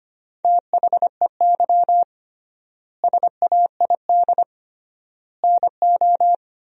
Morse code, heard as a keyed tone, about 25 wpm, 700 Hz. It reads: THEY SAID NO
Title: 3x Word-Spacing - Sentences from Top 100 Words - 25wpm